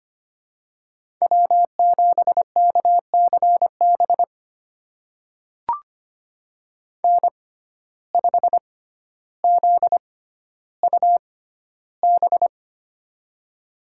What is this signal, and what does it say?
Morse code, heard as a keyed tone, about 25 wpm, spaced as for 13 wpm, 700 Hz. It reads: W7KCB N5ZUB
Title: USA Call Signs - 5 Characters - 6x Character Spacing - 25wpm